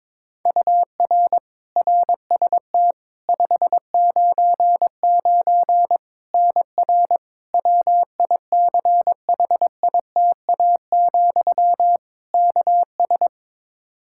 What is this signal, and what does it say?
Morse code, heard as a keyed tone, about 22 wpm, 700 Hz. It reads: UR RST 599 NR WICHITA, KS
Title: K1USN SST QSO Practice 22wpm